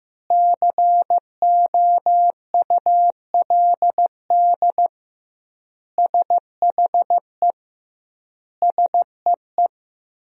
Morse code, heard as a keyed tone, about 15 wpm, 700 Hz. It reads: COULD SHE SEE
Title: 2x Word-Spacing - Sentences from Top 100 Words - 15wpm